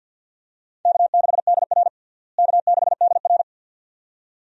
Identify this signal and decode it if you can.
Morse code, heard as a keyed tone, about 53 wpm, 700 Hz. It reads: K6DR K6DR
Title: USA Call Signs - 4 Characters - Repeated 2x - 50wpm